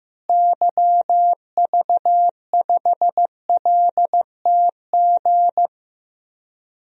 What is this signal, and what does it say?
Morse code, heard as a keyed tone, about 15 wpm, 700 Hz. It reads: YV5LTG
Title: DXCC-100 Call Signs - 6 Characters - Repeated 3x - 15wpm